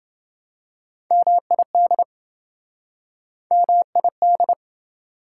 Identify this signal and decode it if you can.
Morse code, heard as a keyed tone, about 26 wpm, 700 Hz. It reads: MID MID